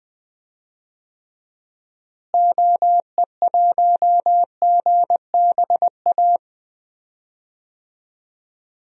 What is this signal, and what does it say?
Morse code, heard as a keyed tone, about 20 wpm, 700 Hz. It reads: OE1GBA
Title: DXCC-100 Call Signs - 6 Characters - Repeated 3x - 20wpm